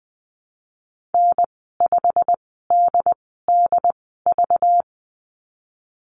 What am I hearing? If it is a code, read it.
Morse code, heard as a keyed tone, about 20 wpm, spaced as for 15 wpm, 700 Hz. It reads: N5DDV
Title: 238 Lesson 38g - Letter q - Call Signs 20wpm